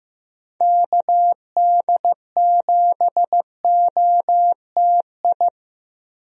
Morse code, heard as a keyed tone, about 15 wpm, 700 Hz. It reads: KD7OTI
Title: USA Call Signs - 6 Characters - Repeated 2x - 15wpm